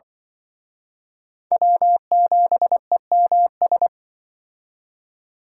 Morse code, heard as a keyed tone, about 24 wpm, 700 Hz. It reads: W7EMS
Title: Speed Racing - USA 1x3 Call Signs 20wpm